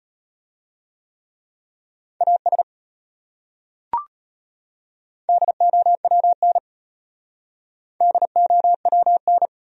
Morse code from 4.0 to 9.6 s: DOWN DOWN

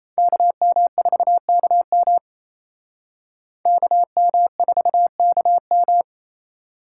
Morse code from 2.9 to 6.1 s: KM4KM

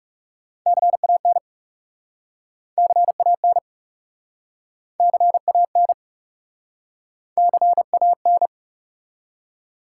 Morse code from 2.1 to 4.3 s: CAN

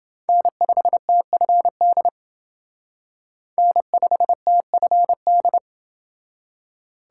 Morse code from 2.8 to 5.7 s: N5TFD

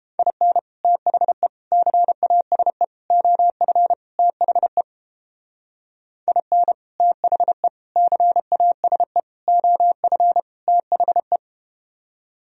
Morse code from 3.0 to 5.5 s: OF THE

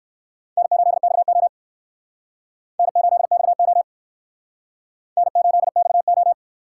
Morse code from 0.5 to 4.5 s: N8XO N8XO

